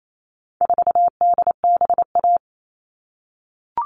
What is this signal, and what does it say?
Morse code, heard as a keyed tone, about 28 wpm, 700 Hz. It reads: 4DBA
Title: All Cut Numbers - 4-Digit 28wpm